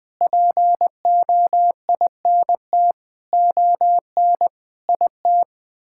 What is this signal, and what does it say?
Morse code, heard as a keyed tone, about 21 wpm, 700 Hz. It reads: POINT ON IT
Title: Sets of 9 Words - 20wpm